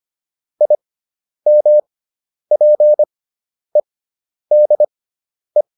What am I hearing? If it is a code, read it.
Morse code, heard as a keyed tone, about 25 wpm, spaced as for 10 wpm, 600 Hz. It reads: IMPEDE
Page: intermediate-practice-files